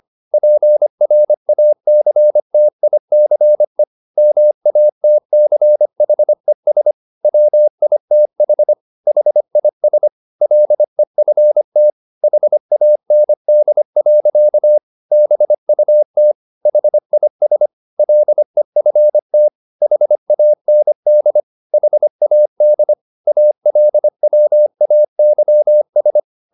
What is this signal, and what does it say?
Morse code, heard as a keyed tone, about 25 wpm, 600 Hz. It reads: PRACTICE MATCHES WITH HIS LEFT HAND. BUT HIS LEFT HAND HAD ALWAYS